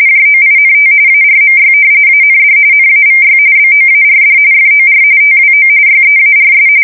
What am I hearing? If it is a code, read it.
RTTY, about 45.45 baud. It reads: JOINT ARMY AND NAVY BOARD FOR TWO YEARS
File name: fsk170 / usnavy4